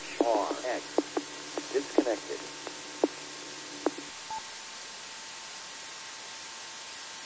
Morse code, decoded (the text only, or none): none